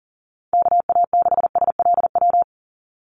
Morse code, heard as a keyed tone, about 40 wpm, 700 Hz. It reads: KA6SLW
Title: USA Call Signs 40wpm